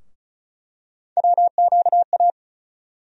Morse code, heard as a keyed tone, about 36 wpm, 700 Hz. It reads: WQA
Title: USA Call Sign Postfixes - 3 Letters 35wpm